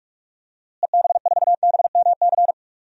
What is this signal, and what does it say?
Morse code, heard as a keyed tone, about 46 wpm, 700 Hz. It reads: EB4BMC
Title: DXCC-100 Call Signs - 6 Characters - Repeated 3x - 45wpm